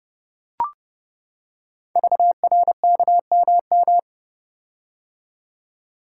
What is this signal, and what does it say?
Morse code, heard as a keyed tone, about 28 wpm, 690 Hz. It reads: VRKMM
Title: Code Groups - 5 Letters - 30wpm